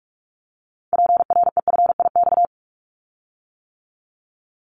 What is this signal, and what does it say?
Morse code, heard as a keyed tone, about 46 wpm, 700 Hz. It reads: PREFIX
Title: Callsign Prefixes for Top 100 DXCC - 45wpm